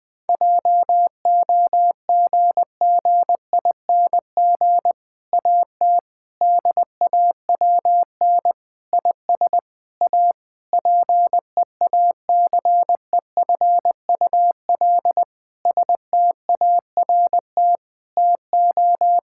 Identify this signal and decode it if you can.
Morse code, heard as a keyed tone, about 20 wpm, 700 Hz. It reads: JOGGING AT DAWN IS A PEACEFUL START TO
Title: Difficult Sentences with Hints - 20wpm